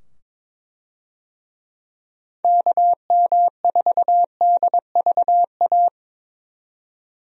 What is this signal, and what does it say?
Morse code, heard as a keyed tone, about 21 wpm, 700 Hz. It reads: KM4DVA
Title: USA Call Signs - 6 Characters - 22wpm